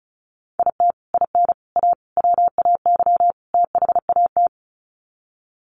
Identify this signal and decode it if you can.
Morse code, heard as a keyed tone, about 34 wpm, 700 Hz. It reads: IT IN A WAY THAT
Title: Sets of 5 Words Encore2 35wpm